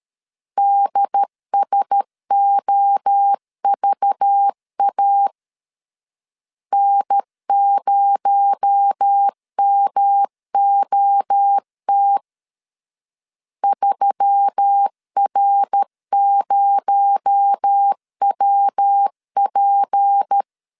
Morse code, heard as a keyed tone, about 13 wpm, 800 Hz. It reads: DSOVA N0MOT 3R0WP
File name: CW / DET / 1 6 1500C S A 12wpm 3440